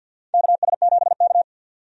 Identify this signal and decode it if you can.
Morse code, heard as a keyed tone, about 49 wpm, 700 Hz. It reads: KS7K
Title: USA Call Signs - 4 Characters - 50wpm